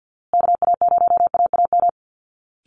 Morse code, heard as a keyed tone, about 50 wpm, 700 Hz. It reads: KA0AAM
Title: Call Signs Repeated 2x - 50wpm